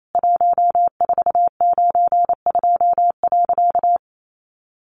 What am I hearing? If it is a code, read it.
Morse code, heard as a keyed tone, about 28 wpm, 700 Hz. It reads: 1492.
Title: A Sound Of Thunder 28wpm